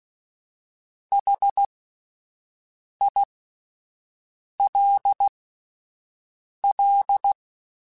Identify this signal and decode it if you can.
Morse code, heard as a keyed tone, about 16 wpm, spaced as for 5 wpm, 800 Hz. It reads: HILL
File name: (spaced) / QSOs Spaced-mp3s / kc7haa @10 sp